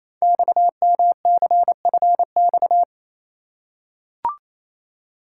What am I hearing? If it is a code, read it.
Morse code, heard as a keyed tone, about 29 wpm, 700 Hz. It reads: XMCFX
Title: Code Groups - 5 Letters - 28wpm